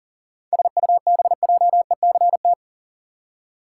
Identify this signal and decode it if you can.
Morse code, heard as a keyed tone, about 40 wpm, 700 Hz. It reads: SUBJECT
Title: Seven Letter Words - 40wpm